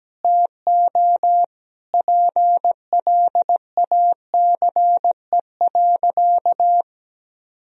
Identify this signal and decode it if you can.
Morse code, heard as a keyed tone, about 17 wpm, 700 Hz. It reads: TO PLACE.
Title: Sentences from Top 1000 Words - Review Part 1 - 17wpm